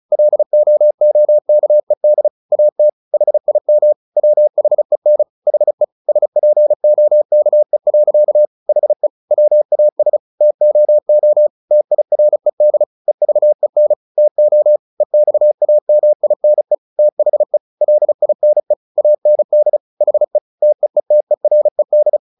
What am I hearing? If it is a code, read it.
Morse code, heard as a keyed tone, about 34 wpm, 600 Hz. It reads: LOOKED AT HIM WHEN HE SPOKE. HE WAS TOO TIRED EVEN TO EXAMINE THE LINE AND HE TEETERED
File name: the old man and the sea / the old man and the sea 1 35wpm